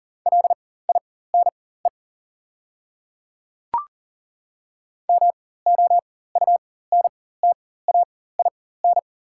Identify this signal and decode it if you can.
Morse code, heard as a keyed tone, about 40 wpm, spaced as for 19 wpm, 690 Hz. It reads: LINE MOUNTAIN
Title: 4x Character Spacing - Top 500 Words - 40wpm